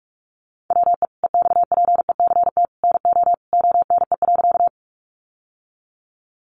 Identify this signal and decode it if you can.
Morse code, heard as a keyed tone, about 44 wpm, 700 Hz. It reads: WE EXPECT NO ONE.
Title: Sentences from Top 1100 Words 45wpm